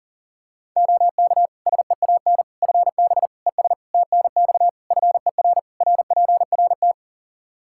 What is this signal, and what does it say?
Morse code, heard as a keyed tone, about 37 wpm, 700 Hz. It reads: OK SEAN FB ES TNX FER RPRT
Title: Practice QSO - Exchange 2 - Info - 40wpm